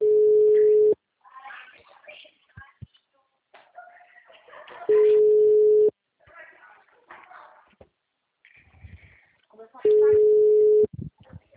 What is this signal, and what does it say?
Ringback tone (Europe, 425 Hz), with speech.